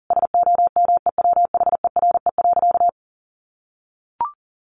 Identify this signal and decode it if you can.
Morse code, heard as a keyed tone, about 37 wpm, 700 Hz. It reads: SOMEWHERE.
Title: Sentences from Top 1400 Words 40wpm